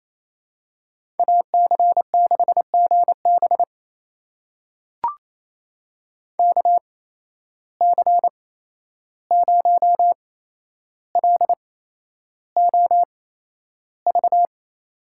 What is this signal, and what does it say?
Morse code, heard as a keyed tone, about 28 wpm, spaced as for 11 wpm, 700 Hz. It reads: AC6GB KC0LOV
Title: USA Call Signs - 8x Character Spacing - 28wpm